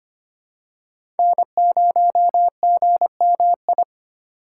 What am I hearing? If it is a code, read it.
Morse code, heard as a keyed tone, about 25 wpm, 700 Hz. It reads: N0GMI
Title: USA Call Signs - 5 Characters - 3x Character Spacing - 25wpm